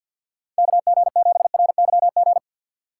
Morse code, heard as a keyed tone, about 65 wpm, spaced as for 57 wpm, 700 Hz. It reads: KG7RYG
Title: Speed Racing - USA Call Signs 50wpm